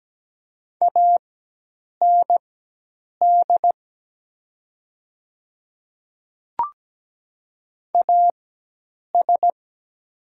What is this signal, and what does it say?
Morse code, heard as a keyed tone, about 17 wpm, spaced as for 8 wpm, 700 Hz. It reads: AND AS
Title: 4x Character Spacing - Top 100 Words - 17wpm